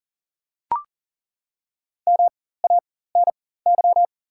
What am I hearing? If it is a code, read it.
Morse code, heard as a keyed tone, about 40 wpm, spaced as for 19 wpm, 700 Hz. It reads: MANY
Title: 4x Character Spacing - Top 100 Words - 40wpm